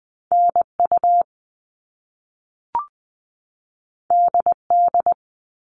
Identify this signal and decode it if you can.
Morse code, heard as a keyed tone, about 20 wpm, 700 Hz. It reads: NU DD
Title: All Cut Numbers - 2-Digit 20wpm